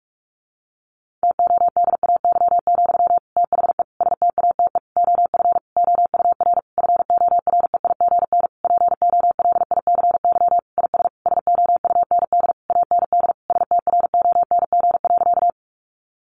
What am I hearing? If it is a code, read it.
Morse code, heard as a keyed tone, about 44 wpm, 700 Hz. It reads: TODAY, THE STATE OF OUR FOREIGN POLICY IS SOUND AND STRONG.
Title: Sentences from Top 1400 Words 45wpm